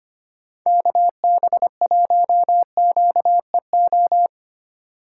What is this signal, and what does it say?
Morse code, heard as a keyed tone, about 25 wpm, 700 Hz. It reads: KB1QEO
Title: USA Call Signs - 6 Characters - 3x Character Spacing - 25wpm